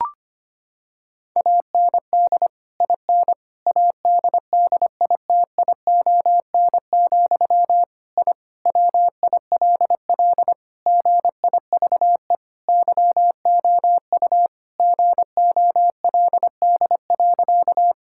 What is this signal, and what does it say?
Morse code, heard as a keyed tone, about 25 wpm, 700 Hz. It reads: AND IN ADDITION, I WILL GIVE YOU GOLD.
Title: Sentences from Top 1000 Words - Review Part 6 - 25wpm